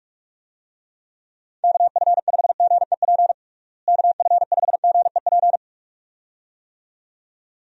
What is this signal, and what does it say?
Morse code, heard as a keyed tone, about 46 wpm, 700 Hz. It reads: KF5GEP KF5GEP
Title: USA Call Signs - 6 Characters - Repeated 2x - 45wpm